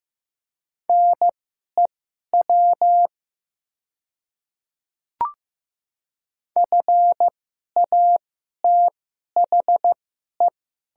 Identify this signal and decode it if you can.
Morse code, heard as a keyed tone, about 15 wpm, spaced as for 11 wpm, 700 Hz. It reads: NEW FATHE
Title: 2x Character Spacing - Top 200 Words - 15wpm